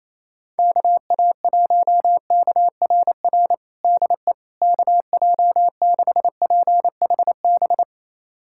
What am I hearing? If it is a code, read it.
Morse code, heard as a keyed tone, about 28 wpm, 700 Hz. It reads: KA1KRR DE KJ6PHB
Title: USA Callsign DE Callsign - 6 Characters - 28wpm